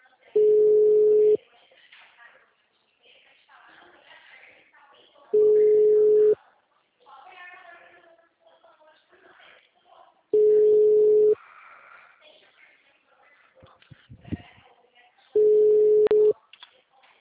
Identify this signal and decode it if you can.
Ringback tone (Europe, 425 Hz), with speech.